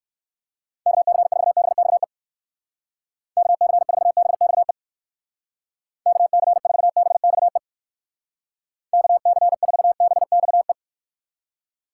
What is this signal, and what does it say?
Morse code, heard as a keyed tone, about 58 wpm, 700 Hz. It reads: KC4BXE KC4BXE KC4BXE KC4BXE